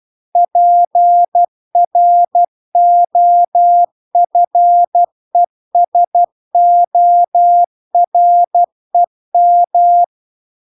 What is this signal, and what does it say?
Morse code, heard as a keyed tone, about 12 wpm, 700 Hz. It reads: PROFESOREM